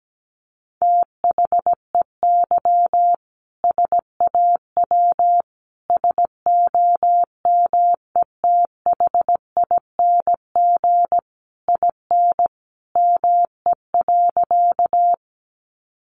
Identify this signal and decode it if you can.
Morse code, heard as a keyed tone, about 17 wpm, 700 Hz. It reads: THEY SAW SOMETHING IN ME.
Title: Sentences from Top 600 Words 17wpm